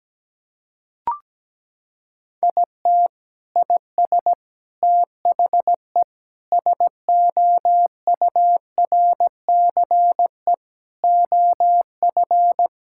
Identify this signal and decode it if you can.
Morse code, heard as a keyed tone, about 17 wpm, 700 Hz. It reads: IT IS THE SOURCE OF
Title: Sentences from Top 1000 Words - Review Part 5 - 17wpm